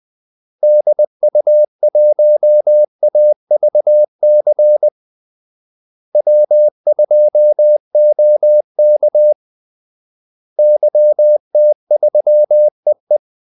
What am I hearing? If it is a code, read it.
Morse code, heard as a keyed tone, about 20 wpm, 600 Hz. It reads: DU1AVC W2OK YT3EE
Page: Kaldesignaler 20 wpm